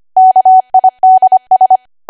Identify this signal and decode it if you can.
Morse code, heard as a keyed tone, about 25 wpm, 750 Hz. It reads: KIDS
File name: Archive / 25 WPM / 25WPM